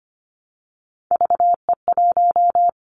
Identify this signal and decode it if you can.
Morse code, heard as a keyed tone, about 25 wpm, 700 Hz. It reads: VE1
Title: Callsign Prefixes for Top 100 DXCC - 25wpm